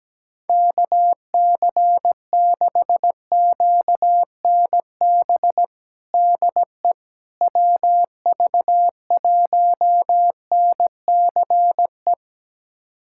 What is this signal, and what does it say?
Morse code, heard as a keyed tone, about 17 wpm, 700 Hz. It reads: KC6QNB DE WV1NCE
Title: USA Callsign DE Callsign - 6 Characters - 17wpm